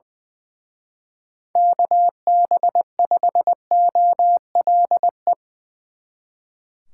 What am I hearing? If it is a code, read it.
Morse code, heard as a keyed tone, about 22 wpm, 700 Hz. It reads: KB5OLE
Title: Speed Racing - USA 2x3 Call Signs 20wpm